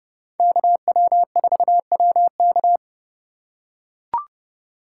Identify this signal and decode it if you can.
Morse code, heard as a keyed tone, about 31 wpm, 700 Hz. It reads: KW4WK
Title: USA Call Signs - 5 Characters - 2x Character Spacing - 30wpm